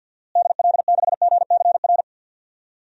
Morse code, heard as a keyed tone, about 51 wpm, 700 Hz. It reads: DL6GCR
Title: DXCC-100 Call Signs - 6 Characters - Repeated 3x - 50wpm